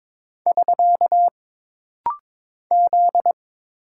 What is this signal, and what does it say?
Morse code, heard as a keyed tone, about 22 wpm, 700 Hz. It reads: <SK> Z
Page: This rapid-fire timing is geared to push your ICR (Instant Character Recognition) skills.